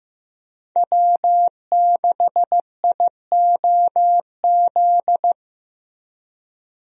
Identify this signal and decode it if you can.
Morse code, heard as a keyed tone, about 16 wpm, 700 Hz. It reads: W6IOZ
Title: USA Call Signs - 5 Characters - Repeated 2x - 15wpm